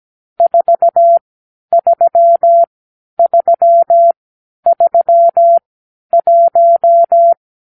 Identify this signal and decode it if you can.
Morse code, heard as a keyed tone, about 17 wpm, spaced as for 10 wpm, 680 Hz. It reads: 43331
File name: k7qo / Lesson 46 (1-4)